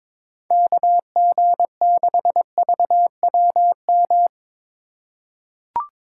Morse code, heard as a keyed tone, about 21 wpm, 700 Hz. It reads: KG6VWM
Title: USA Call Signs - 6 Characters - 2x Character Spacing - 22wpm